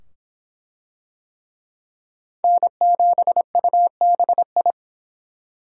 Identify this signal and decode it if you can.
Morse code, heard as a keyed tone, about 24 wpm, 700 Hz. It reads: N7UBI